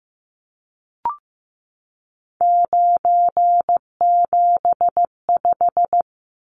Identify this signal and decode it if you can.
Morse code, heard as a keyed tone, about 15 wpm, 700 Hz. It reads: 975
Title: Numbers - 3-Digit 15wpm